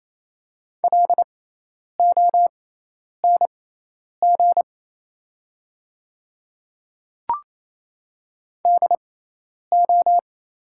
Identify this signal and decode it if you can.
Morse code, heard as a keyed tone, about 28 wpm, spaced as for 10 wpm, 700 Hz. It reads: LONG DO
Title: 6x Character Spacing - Top 100 Words - 28wpm